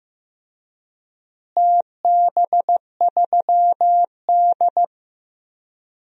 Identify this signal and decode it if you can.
Morse code, heard as a keyed tone, about 15 wpm, 700 Hz. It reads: TB3D
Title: DXCC-100 Call Signs - 4 Characters - Repeated 2x - 15wpm